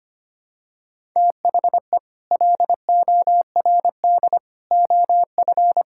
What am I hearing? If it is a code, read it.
Morse code, heard as a keyed tone, about 25 wpm, 700 Hz. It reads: THE LORD OF
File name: Sets of 9 Words - 25wpm